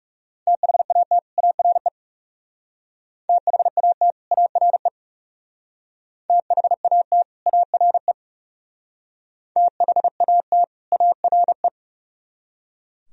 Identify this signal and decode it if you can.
Morse code, heard as a keyed tone, about 36 wpm, 700 Hz. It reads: THAT ARE THAT ARE THAT ARE THAT ARE
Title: Speed-Racing - Sets of 2 Words 30wpm